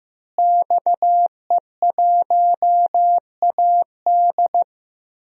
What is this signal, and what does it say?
Morse code, heard as a keyed tone, about 15 wpm, 700 Hz. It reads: XE1AD
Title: DXCC-100 Call Signs - 5 Characters - Repeated 2x - 15wpm